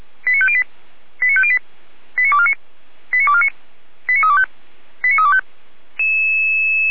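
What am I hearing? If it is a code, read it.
Five-tone selective call (ZVEI), heard as quick runs of five tones, tones of about 70 ms. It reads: 89508 89508 89249 89249 89325 89325
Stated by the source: ZVEI five-tone call received by SDR, from the server of feuerwehr-orlamuende (fire brigade Orlamünde).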